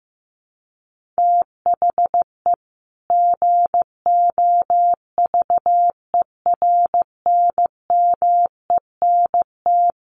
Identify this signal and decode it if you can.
Morse code, heard as a keyed tone, about 14 wpm, 700 Hz. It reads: THE GOVERNMENT
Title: Sentences from Top 1500 Words 15wpm